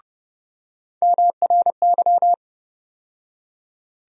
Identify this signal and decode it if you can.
Morse code, heard as a keyed tone, about 30 wpm, 700 Hz. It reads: MRY
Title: USA Call Sign Postfixes - 3 Letters 30wpm